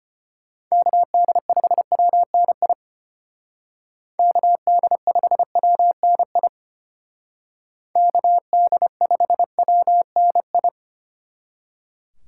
Morse code, heard as a keyed tone, about 32 wpm, 700 Hz. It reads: KD5WNI KD5WNI KD5WNI